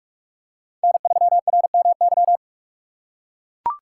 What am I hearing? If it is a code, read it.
Morse code, heard as a keyed tone, about 42 wpm, 700 Hz. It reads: N3RMY